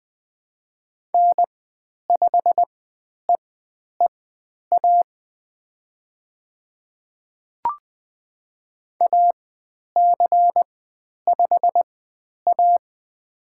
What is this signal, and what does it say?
Morse code, heard as a keyed tone, about 20 wpm, spaced as for 10 wpm, 700 Hz. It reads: N5EEA AC5A
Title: 109 Lesson 18g - Letter m - Call Signs 20wpm